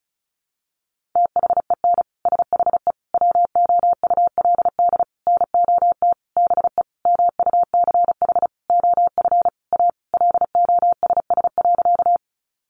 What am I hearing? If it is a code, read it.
Morse code, heard as a keyed tone, about 34 wpm, 700 Hz. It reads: THEN SHE WOULD NOT BE MUCH OF A LOSS.
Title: Sentences from Top 1400 Words 35wpm